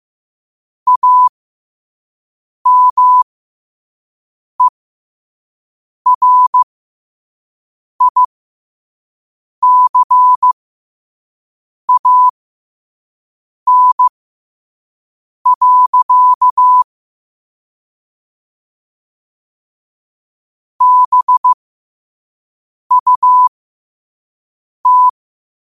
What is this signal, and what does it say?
Morse code, heard as a keyed tone, about 15 wpm, spaced as for 5 wpm, 1000 Hz. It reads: AMERICAN. BUT
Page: New quotes every day in morse code at 5 Words per minute.